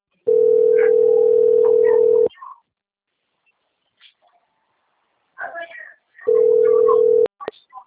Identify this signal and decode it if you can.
Ringback tone (North America), 440 + 480 Hz, with speech.
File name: out- - - - - .88503